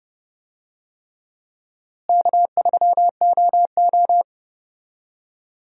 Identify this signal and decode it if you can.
Morse code, heard as a keyed tone, about 30 wpm, 700 Hz. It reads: K3OO
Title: USA Call Signs - 4 Characters - Repeated 3x - 30wpm